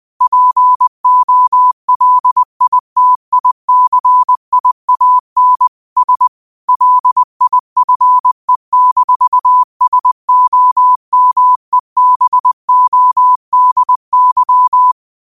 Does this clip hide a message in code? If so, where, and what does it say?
Yes: Morse code from 0.2 to 14.9 s, heard as a keyed tone, about 20 wpm, 1000 Hz. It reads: POLITICIAN S LIFE-SOMEBODY